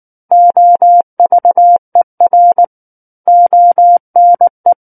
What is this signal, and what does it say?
Morse code, heard as a keyed tone, about 19 wpm, 700 Hz. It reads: OVER ONE
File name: Book One / 10-Chapter 10 - 19 WPM